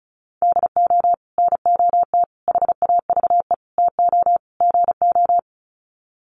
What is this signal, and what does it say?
Morse code, heard as a keyed tone, about 34 wpm, 700 Hz. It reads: DO NOT HAVE TO GO
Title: Sets of 5 Words 35wpm